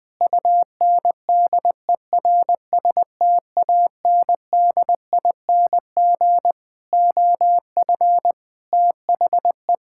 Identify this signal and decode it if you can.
Morse code, heard as a keyed tone, about 20 wpm, 700 Hz. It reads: UNDERSTANDING OF THE
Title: Sentences from Top 1000 Words - Review Part 5 - 20wpm